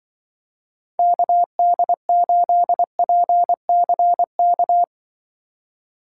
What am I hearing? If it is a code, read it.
Morse code, heard as a keyed tone, about 23 wpm, 700 Hz. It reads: KD8PCK
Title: Speed Racing - USA 2x3 Call Signs 20wpm